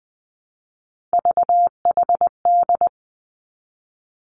Morse code, heard as a keyed tone, about 20 wpm, 700 Hz. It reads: VHD
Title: Three Challenging Letter-Number Combinations 20wpm